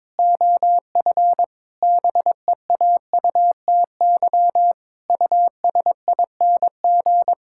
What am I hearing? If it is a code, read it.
Morse code, heard as a keyed tone, about 22 wpm, 700 Hz. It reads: OF BEAUTY USING
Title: Pangram Challenge v2 - 22wpm